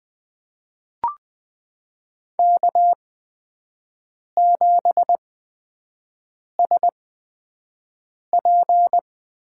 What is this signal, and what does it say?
Morse code, heard as a keyed tone, about 20 wpm, spaced as for 5 wpm, 700 Hz. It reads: K7SP
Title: USA Call Signs - 5 Characters - 8x Character Spacing - 20wpm